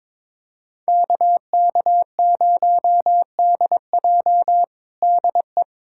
Morse code, heard as a keyed tone, about 22 wpm, 700 Hz. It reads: KK0DJ DE
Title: USA Callsign DE Callsign - 5 Characters - 22wpm